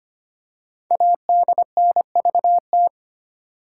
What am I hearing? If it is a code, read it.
Morse code, heard as a keyed tone, about 25 wpm, 700 Hz. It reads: ADNVT
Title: Code Groups - 5 Letters - 25wpm